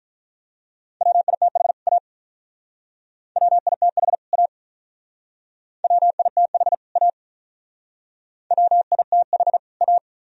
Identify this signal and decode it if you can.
Morse code, heard as a keyed tone, about 46 wpm, 700 Hz. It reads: WITH A WITH A WITH A WITH A